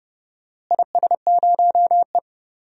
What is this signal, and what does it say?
Morse code, heard as a keyed tone, about 31 wpm, 700 Hz. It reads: IS0E
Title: DXCC-100 Call Signs - 4 Characters - Repeated 2x - 30wpm